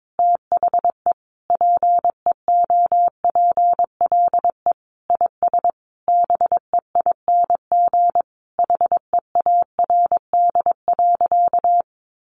Morse code, heard as a keyed tone, about 22 wpm, 700 Hz. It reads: THE PEOPLE IS BEING HEARD.
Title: Sentences from Top 1500 Words 22wpm